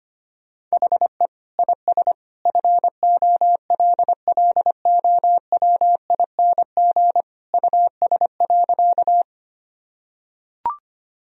Morse code, heard as a keyed tone, about 25 wpm, 700 Hz. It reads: HE IS FOLLOWING US.
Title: Sentences from Top 1000 Words - Review Part 4 - 25wpm